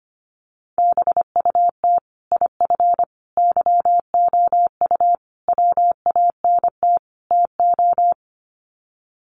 Morse code, heard as a keyed tone, about 25 wpm, 700 Hz. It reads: BUT IF YOU WANT TO
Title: Sets of 5 Words Encore2 25wpm